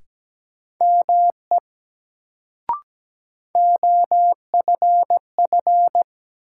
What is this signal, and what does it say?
Morse code, heard as a keyed tone, about 17 wpm, 700 Hz. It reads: ME OFF
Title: Sentences from Top 200 Words - New Word Drill - 17wpm